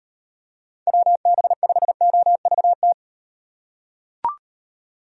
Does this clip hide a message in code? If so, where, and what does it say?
Yes: Morse code from 0.9 to 2.9 s, heard as a keyed tone, about 38 wpm, 700 Hz. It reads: WB5OVT